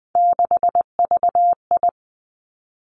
Morse code, heard as a keyed tone, about 20 wpm, 700 Hz. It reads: 6VI
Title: Three Challenging Letter-Number Combinations 20wpm